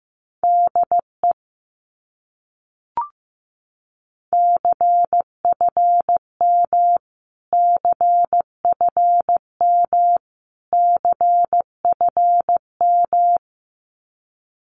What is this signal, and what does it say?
Morse code, heard as a keyed tone, about 15 wpm, 700 Hz. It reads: DE CFM CFM CFM
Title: Abbreviations Repeated 3x - 15wpm